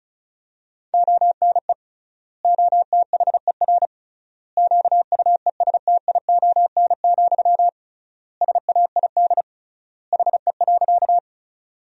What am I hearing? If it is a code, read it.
Morse code, heard as a keyed tone, about 36 wpm, 700 Hz. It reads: ONE OTHER QUESTION, SAID HE.
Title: 3x Word-Spacing - Sentences from Top 400 Words - 35wpm